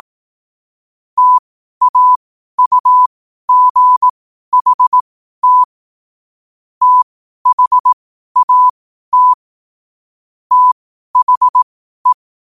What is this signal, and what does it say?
Morse code, heard as a keyed tone, about 18 wpm, spaced as for 12 wpm, 1000 Hz. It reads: TAUGHT THAT THE
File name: qotdfiles / 12wpm